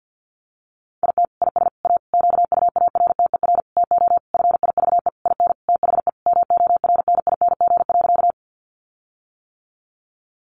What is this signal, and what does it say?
Morse code, heard as a keyed tone, about 50 wpm, 700 Hz. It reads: IT IS A QUARTER TO FIVE IN THE MORNING.
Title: Sentences from Top 1300 Words 50wpm